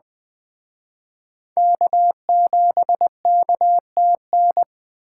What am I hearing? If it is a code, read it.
Morse code, heard as a keyed tone, about 21 wpm, 700 Hz. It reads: K7KTN